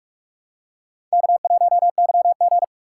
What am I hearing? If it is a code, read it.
Morse code, heard as a keyed tone, about 45 wpm, 700 Hz. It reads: K1YG